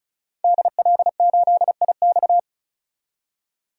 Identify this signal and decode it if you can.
Morse code, heard as a keyed tone, about 36 wpm, 700 Hz. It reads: DL8IX
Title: DXCC-100 Call Signs - 5 Characters - 35wpm